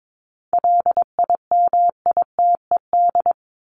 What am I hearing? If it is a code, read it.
Morse code, heard as a keyed tone, about 22 wpm, 700 Hz. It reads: LIMITED
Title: ED Suffix - Top 100 Words with Base Words 22wpm